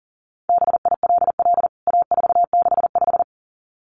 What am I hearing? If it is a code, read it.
Morse code, heard as a keyed tone, about 40 wpm, 700 Hz. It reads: BILL A465
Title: CWT Exchange 40wpm